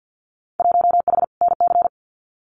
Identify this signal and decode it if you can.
Morse code, heard as a keyed tone, about 50 wpm, 700 Hz. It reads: 1H NC
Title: Winter Field Day Exchanges 50wpm